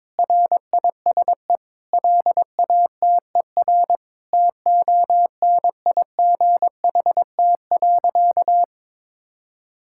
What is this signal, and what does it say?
Morse code, heard as a keyed tone, about 22 wpm, 700 Hz. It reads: RISE LATER TONIGHT.
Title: Sentences from Top 1000 Words - Review Part 6 - 22wpm